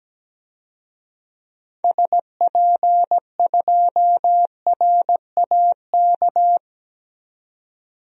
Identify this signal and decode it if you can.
Morse code, heard as a keyed tone, about 17 wpm, 700 Hz. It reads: SP2RAK
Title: DXCC-100 Call Signs - 6 Characters - 17wpm